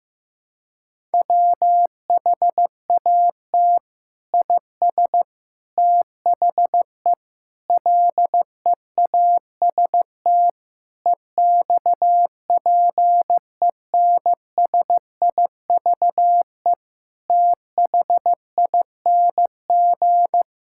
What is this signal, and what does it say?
Morse code, heard as a keyed tone, about 15 wpm, 700 Hz. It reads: WHAT IS THE LEAST EXPENSIVE THING
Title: Sentences from Top 1000 Words - Review Part 6 - 15wpm